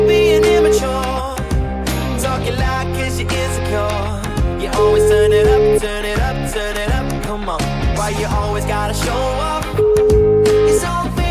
Ringback tone (Europe, 425 Hz), with noise.